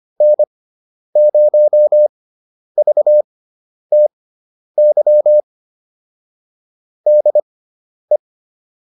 Morse code, heard as a keyed tone, about 25 wpm, spaced as for 10 wpm, 600 Hz. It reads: N0VTY DE